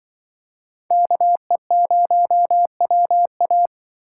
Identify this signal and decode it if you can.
Morse code, heard as a keyed tone, about 25 wpm, 700 Hz. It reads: KE0WA
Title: Speed Racing - USA 2x2 Call Signs 20wpm